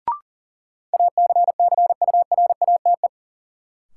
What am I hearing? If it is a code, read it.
Morse code, heard as a keyed tone, about 40 wpm, 700 Hz. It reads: ACCURATE
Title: Sentences from Top 1200 Words - New Word Drill Alt - 40wpm